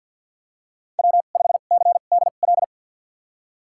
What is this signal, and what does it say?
Morse code, heard as a keyed tone, about 52 wpm, spaced as for 36 wpm, 700 Hz. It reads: W5CDL
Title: USA Call Signs - 5 Characters - 2x Character Spacing - 50wpm